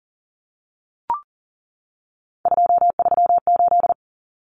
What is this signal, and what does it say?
Morse code, heard as a keyed tone, about 40 wpm, 700 Hz. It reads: 238
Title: Common Cut Numbers - 3-Digit 40wpm